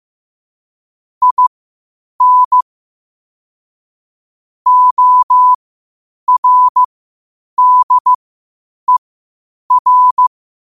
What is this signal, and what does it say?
Morse code, heard as a keyed tone, about 15 wpm, spaced as for 8 wpm, 1000 Hz. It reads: IN ORDER
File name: qotdfiles / 8wpm